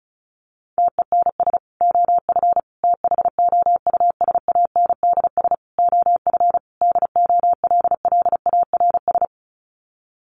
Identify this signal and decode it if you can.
Morse code, heard as a keyed tone, about 34 wpm, 700 Hz. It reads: TENS OF THOUSANDS OF DOLLARS
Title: Sets of 5 Words Encore2 35wpm